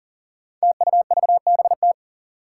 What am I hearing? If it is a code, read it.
Morse code, heard as a keyed tone, about 42 wpm, 700 Hz. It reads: TUVBT
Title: Five Challenging Letter-Number Combinations 40wpm